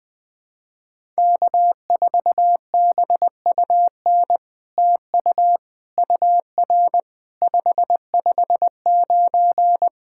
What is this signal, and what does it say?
Morse code, heard as a keyed tone, about 20 wpm, 700 Hz. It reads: K4BUN TU UR 559
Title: SOTA QSO Practice 20wpm